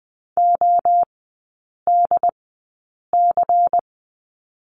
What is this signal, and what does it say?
Morse code, heard as a keyed tone, about 20 wpm, spaced as for 8 wpm, 700 Hz. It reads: ODC
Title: 138 Lesson 23e - Letter f - Three Characters 20wpm